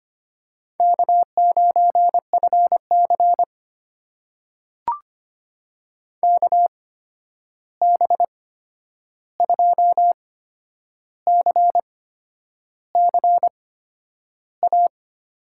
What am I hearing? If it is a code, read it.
Morse code, heard as a keyed tone, about 25 wpm, spaced as for 9 wpm, 700 Hz. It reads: K9FC KB2CCA
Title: USA Call Signs - 8x Character Spacing - 25wpm